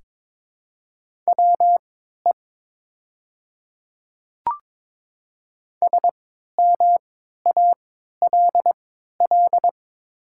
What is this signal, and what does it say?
Morse code, heard as a keyed tone, about 22 wpm, spaced as for 13 wpm, 700 Hz. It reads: WE SMALL
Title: 3x Character Spacing - Top 200 Words - 22wpm